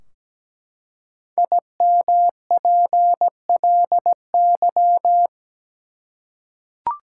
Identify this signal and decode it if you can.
Morse code, heard as a keyed tone, about 17 wpm, 700 Hz. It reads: IMPLY
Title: Five Letter Words - 17wpm